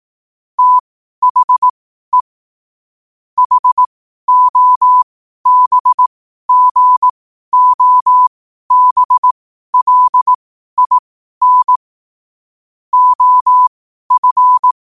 Morse code, heard as a keyed tone, about 18 wpm, spaced as for 12 wpm, 1000 Hz. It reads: THE HOBGOBLIN OF
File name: qotdfiles / 12wpm